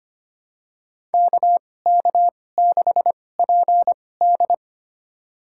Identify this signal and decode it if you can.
Morse code, heard as a keyed tone, about 25 wpm, spaced as for 18 wpm, 700 Hz. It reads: KK6PD
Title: USA Call Signs - 2x Character Spacing - 25wpm